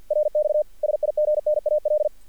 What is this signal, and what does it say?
Morse code, heard as a keyed tone, about 48 wpm, 600 Hz. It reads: RX SIGNAL
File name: archived / CRSnet